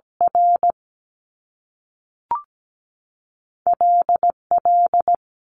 Morse code, heard as a keyed tone, about 17 wpm, 700 Hz. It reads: R L L
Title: Top Two Letter Combinations in Words 17wpm